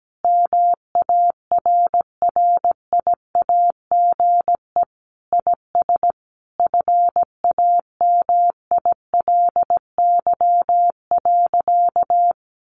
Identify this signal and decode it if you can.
Morse code, heard as a keyed tone, about 17 wpm, 700 Hz. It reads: MARRIAGE IS FAMILY.
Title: Sentences from Top 800 Words 17wpm